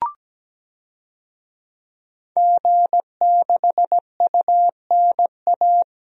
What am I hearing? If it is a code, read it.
Morse code, heard as a keyed tone, about 17 wpm, 700 Hz. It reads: G6UNA
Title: DXCC-100 Call Signs - 5 Characters - 17wpm